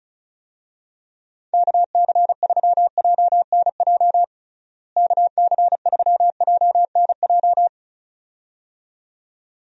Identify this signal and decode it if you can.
Morse code, heard as a keyed tone, about 36 wpm, 700 Hz. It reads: KC3JNJ KC3JNJ
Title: USA Call Signs - 6 Characters - Repeated 2x - 35wpm